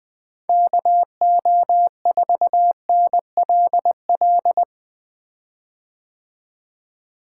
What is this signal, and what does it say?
Morse code, heard as a keyed tone, about 20 wpm, 700 Hz. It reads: KO4NLL
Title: 2023 POTA Activator Call Signs - 20wpm